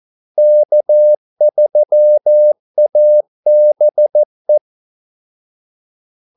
Morse code, heard as a keyed tone, about 14 wpm, 600 Hz. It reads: K3ABE
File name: Callsigns-14wpm